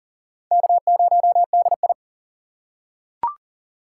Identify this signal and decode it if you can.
Morse code, heard as a keyed tone, about 40 wpm, 700 Hz. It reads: K0DI
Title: USA Call Signs - 4 Characters - 4x Character Spacing - 40wpm